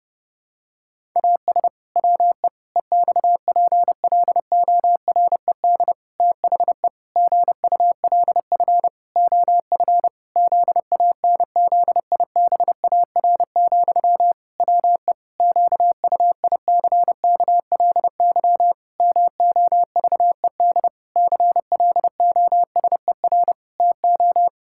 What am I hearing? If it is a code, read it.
Morse code, heard as a keyed tone, about 30 wpm, 700 Hz. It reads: AS WE EXPLORED THE GULF OF ZANZIBAR, WE QUICKLY MOVED CLOSER TO
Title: Pangram Challenge v2 - 30wpm